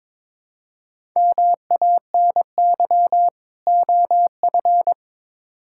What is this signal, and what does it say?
Morse code, heard as a keyed tone, about 22 wpm, 700 Hz. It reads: MANY OF
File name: Build A Phrase - 2-Word Phrases - 22wpm